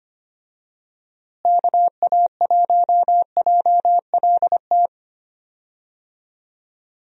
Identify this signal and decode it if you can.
Morse code, heard as a keyed tone, about 25 wpm, 700 Hz. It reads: KA1JLT
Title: USA Call Signs - 6 Characters - Repeated 2x - 25wpm